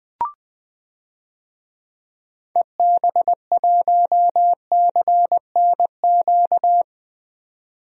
Morse code, heard as a keyed tone, about 20 wpm, 700 Hz. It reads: EB1CNQ
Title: DXCC-100 Call Signs - 6 Characters - 20wpm